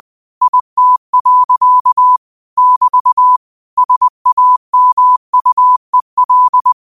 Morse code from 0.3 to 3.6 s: IT. <BT>

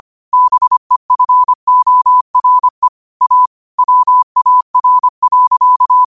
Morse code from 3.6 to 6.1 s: WAR.

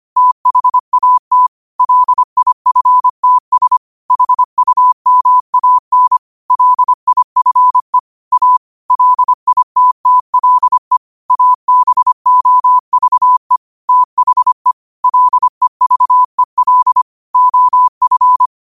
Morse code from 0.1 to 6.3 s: THAT LIFTS HUMAN

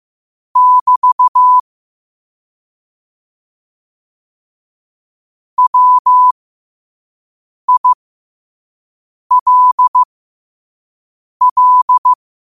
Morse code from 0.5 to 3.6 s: <BT>

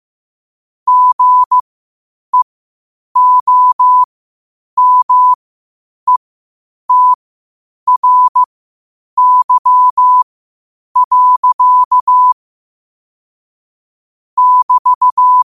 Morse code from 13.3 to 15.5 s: <BT>

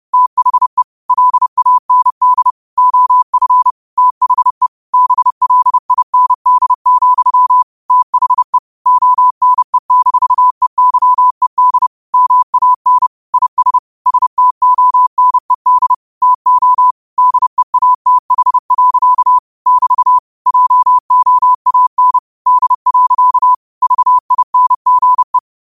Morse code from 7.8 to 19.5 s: THE ONE-EYED MAN IS STONED TO DEATH.

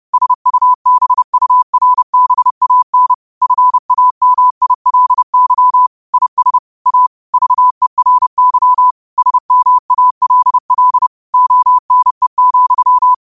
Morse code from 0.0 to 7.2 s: SUBURBAN FAMILY IS A

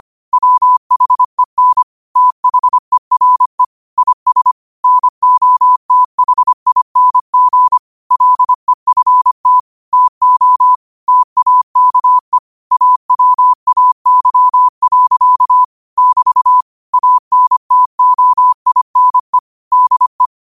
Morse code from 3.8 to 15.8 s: IS NOTHING LEFT TO TAKE AWAY.

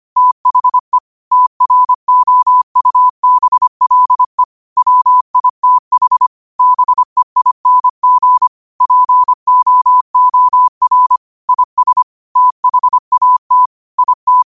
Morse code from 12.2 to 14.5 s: THAT IT

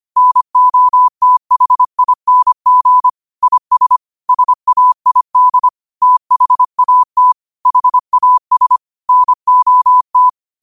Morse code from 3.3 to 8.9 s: IS SAID THAT HAS

